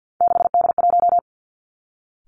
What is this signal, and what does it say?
Morse code, heard as a keyed tone, about 50 wpm, 700 Hz. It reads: 6D1
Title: Three Challenging Letter-Number Combinations 50wpm